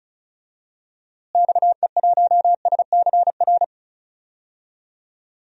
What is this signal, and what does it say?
Morse code, heard as a keyed tone, about 36 wpm, 700 Hz. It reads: XE1SCR
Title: DXCC-100 Call Signs - 6 Characters - Repeated 3x - 35wpm